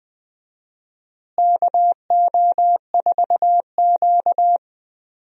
Morse code, heard as a keyed tone, about 20 wpm, 700 Hz. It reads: KO4Q
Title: USA Call Signs - 4 Characters - Repeated 3x - 20wpm